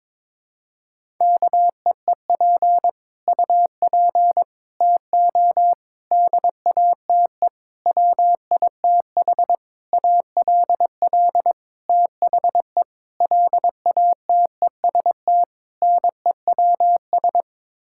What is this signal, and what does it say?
Morse code, heard as a keyed tone, about 22 wpm, 700 Hz. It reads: KEEP UP TO DATE WITH ALL THE LATEST NEWS
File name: Sets of 9 Words - 22wpm